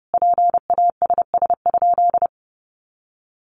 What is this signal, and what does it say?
Morse code, heard as a keyed tone, about 30 wpm, 700 Hz. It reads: PASS?
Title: Sentences from Top 400 Words 30wpm